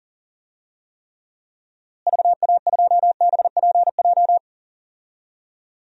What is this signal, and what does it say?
Morse code, heard as a keyed tone, about 40 wpm, 700 Hz. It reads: VA2BPJ
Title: DXCC-100 Call Signs - 6 Characters - Repeated 3x - 40wpm